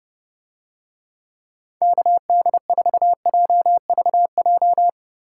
Morse code, heard as a keyed tone, about 29 wpm, 700 Hz. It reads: KD4JVJ